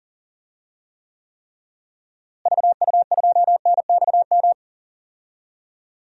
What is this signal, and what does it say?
Morse code, heard as a keyed tone, about 40 wpm, 700 Hz. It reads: VU2NXM
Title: DXCC-100 Call Signs - 6 Characters - Repeated 3x - 40wpm